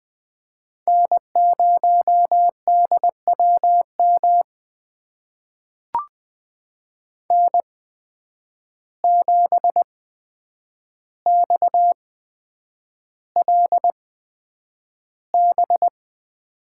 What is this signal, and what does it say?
Morse code, heard as a keyed tone, about 20 wpm, spaced as for 9 wpm, 700 Hz. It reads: N0DWM N7XLB
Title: USA Call Signs - 5 Characters - 8x Character Spacing - 20wpm